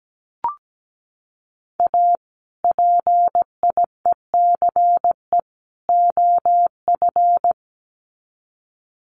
Morse code, heard as a keyed tone, about 17 wpm, 700 Hz. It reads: A PIECE OF
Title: Sets of 3 Words 17wpm